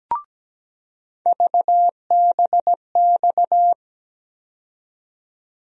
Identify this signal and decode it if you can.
Morse code, heard as a keyed tone, about 17 wpm, 700 Hz. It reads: VBX
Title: USA Call Sign Postfixes - 3 Letters 17wpm